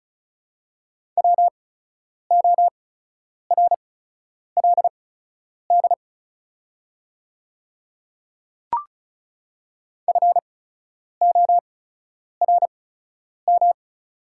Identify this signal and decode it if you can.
Morse code, heard as a keyed tone, about 36 wpm, spaced as for 10 wpm, 700 Hz. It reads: WORLD FORM
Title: 8x Character Spacing - Top 200 Words - 35wpm